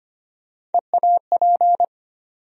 Morse code, heard as a keyed tone, about 25 wpm, 700 Hz. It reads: EAP
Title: USA Call Sign Postfixes - 3 Letters 25wpm